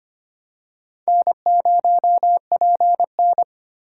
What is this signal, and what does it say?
Morse code, heard as a keyed tone, about 25 wpm, 700 Hz. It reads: N0PN
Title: USA Call Signs - 4 Characters - 2x Character Spacing - 25wpm